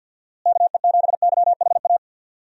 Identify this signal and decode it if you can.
Morse code, heard as a keyed tone, about 51 wpm, 700 Hz. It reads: KE7YHA
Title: USA Call Signs - 6 Characters - Repeated 3x - 50wpm